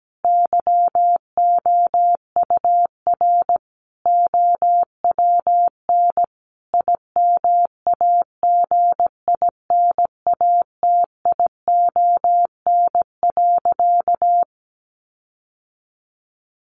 Morse code, heard as a keyed tone, about 17 wpm, 700 Hz. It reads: YOUR OWN IMAGINATION.
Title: Sentences from Top 1300 Words 17wpm